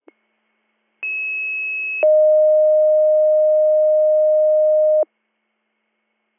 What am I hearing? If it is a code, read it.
Two-tone page, heard as a one-second tone, then a three-second tone, about 2600 then 620 Hz. It is heard from 1.0 to 5.0 s.